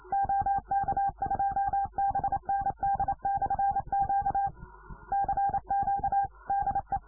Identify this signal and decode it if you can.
Morse code, heard as a keyed tone, about 27 wpm, 800 Hz. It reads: OK2BND/Q CQ DE